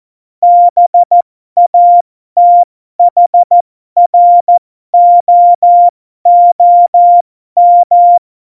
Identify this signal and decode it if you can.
Morse code, heard as a keyed tone, about 14 wpm, spaced as for 12 wpm, 700 Hz. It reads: BATHROOM
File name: Morsekurs / Witze